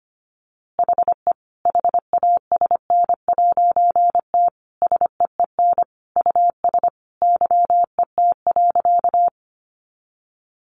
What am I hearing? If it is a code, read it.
Morse code, heard as a keyed tone, about 25 wpm, 700 Hz. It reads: HE HASN'T SEEN US YET.
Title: A Sound Of Thunder 25wpm